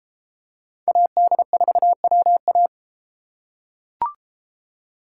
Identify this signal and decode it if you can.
Morse code, heard as a keyed tone, about 33 wpm, 700 Hz. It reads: AD4WA